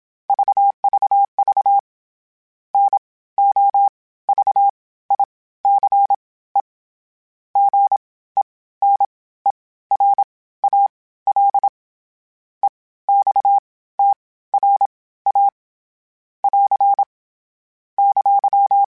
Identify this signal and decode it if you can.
Morse code, heard as a keyed tone, about 27 wpm, spaced as for 16 wpm, 800 Hz. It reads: VVV NOVICE GENERAL EXTRA <AR> !
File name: (spaced) / FromSMs spaced mp3s / Classes sp@25